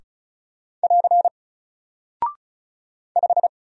Morse code, heard as a keyed tone, about 36 wpm, 700 Hz. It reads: <AR> 5
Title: Single Character (Letter:Number:Punctuation:Prosign) 35wpm